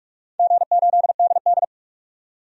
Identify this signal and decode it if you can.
Morse code, heard as a keyed tone, about 46 wpm, 700 Hz. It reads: G8DD
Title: DXCC-100 Call Signs - 4 Characters - 45wpm